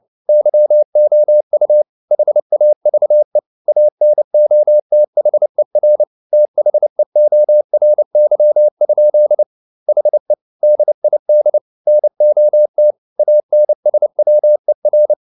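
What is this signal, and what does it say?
Morse code, heard as a keyed tone, about 29 wpm, 600 Hz. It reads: YOU HAVE ANOTHER THEORY? HE DID NOT ANSWER
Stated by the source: Morse code at 29 wpm, sending Ken Follett's novel Never in English, from CW med Gnister.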